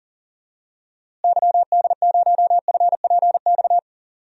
Morse code, heard as a keyed tone, about 42 wpm, 700 Hz. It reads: YD0FPX
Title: DXCC-100 Call Signs - 6 Characters - Repeated 3x - 40wpm